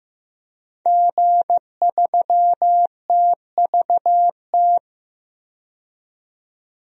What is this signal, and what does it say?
Morse code, heard as a keyed tone, about 15 wpm, 700 Hz. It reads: G3TVT
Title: DXCC-100 Call Signs - 5 Characters - 15wpm